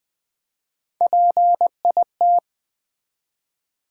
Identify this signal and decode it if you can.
Morse code, heard as a keyed tone, about 20 wpm, 700 Hz. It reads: PIT
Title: Build A Word - 1 Letter At A Time - 4-Letter Words - 20wpm